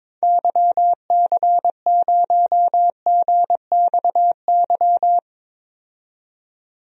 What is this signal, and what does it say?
Morse code, heard as a keyed tone, about 22 wpm, 700 Hz. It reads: YC0GXY
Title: DXCC-100 Call Signs - 6 Characters - 22wpm